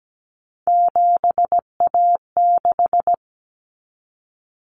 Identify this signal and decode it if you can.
Morse code, heard as a keyed tone, about 17 wpm, 700 Hz. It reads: 7A6
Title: Common Cut Numbers - 3-Digit 17wpm